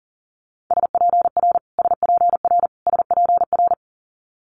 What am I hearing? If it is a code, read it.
Morse code, heard as a keyed tone, about 38 wpm, 700 Hz. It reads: SPR SPR SPR
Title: Multi-letter Phonemes Repeated 3x 40wpm